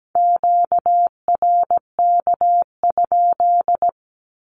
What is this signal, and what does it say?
Morse code, heard as a keyed tone, about 17 wpm, 700 Hz. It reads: QRK?
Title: Q-Codes 17wpm